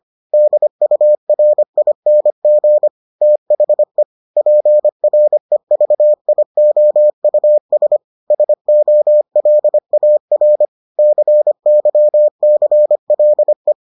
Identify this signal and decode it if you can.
Morse code, heard as a keyed tone, about 25 wpm, 600 Hz. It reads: DURING THE PREVIOUS SOLAR CYCLE